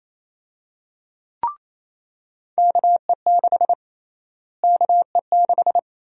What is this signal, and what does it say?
Morse code, heard as a keyed tone, about 28 wpm, 700 Hz. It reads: KE6 KE6
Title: USA Call Sign Prefixes - Repeated 2x - 28wpm